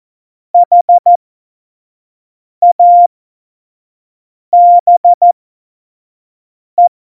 Morse code, heard as a keyed tone, about 14 wpm, spaced as for 5 wpm, 700 Hz. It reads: HABE
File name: Morsekurs / Witze / Witz 25. Juni 2019, 25 F70